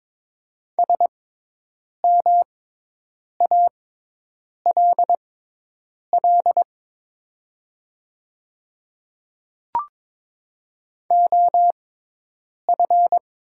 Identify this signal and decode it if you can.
Morse code, heard as a keyed tone, about 22 wpm, spaced as for 8 wpm, 700 Hz. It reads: SMALL OF
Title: 6x Character Spacing - Top 200 Words - 22wpm